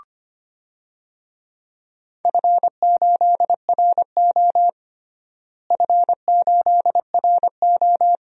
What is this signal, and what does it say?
Morse code, heard as a keyed tone, about 25 wpm, 700 Hz. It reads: F8RO F8RO
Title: DXCC-100 Call Signs - 4 Characters - Repeated 3x - 25wpm